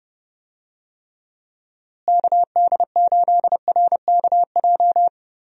Morse code, heard as a keyed tone, about 31 wpm, 700 Hz. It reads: KD8RKJ
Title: USA Call Signs - 6 Characters - Repeated 3x - 30wpm